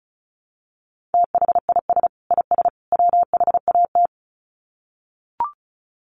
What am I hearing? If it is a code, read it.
Morse code, heard as a keyed tone, about 34 wpm, 700 Hz. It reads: THIS IS WHAT
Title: Sets of 3 Words 35wpm